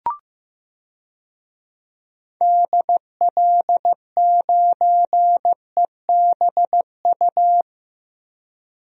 Morse code, heard as a keyed tone, about 15 wpm, 700 Hz. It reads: DL9EBU
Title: DXCC-100 Call Signs - 6 Characters - 15wpm